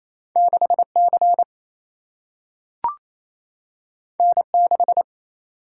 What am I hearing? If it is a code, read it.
Morse code, heard as a keyed tone, about 28 wpm, 700 Hz. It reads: 6C N6
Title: Single-Letter Single-Number 28wpm